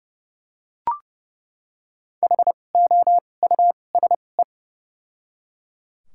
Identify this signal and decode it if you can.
Morse code, heard as a keyed tone, about 30 wpm, 700 Hz. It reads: H O U S E
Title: 2x Character Spacing - Top 200 Words - 30wpm